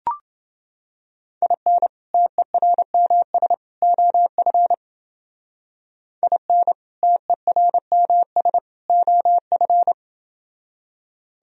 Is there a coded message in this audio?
Yes: Morse code from 1.4 to 9.9 s, heard as a keyed tone, about 29 wpm, 700 Hz. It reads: IN TERMS OF IN TERMS OF